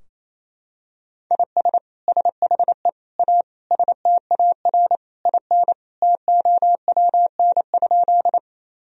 Morse code, heard as a keyed tone, about 28 wpm, 700 Hz. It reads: IS SHE A STAR IN TOWN?
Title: Sentences from Top 1000 Words - Review Part 2 - 28wpm